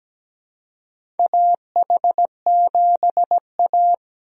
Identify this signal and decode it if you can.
Morse code, heard as a keyed tone, about 17 wpm, 700 Hz. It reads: AH7A
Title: USA Call Signs - 4 Characters - Repeated 2x - 17wpm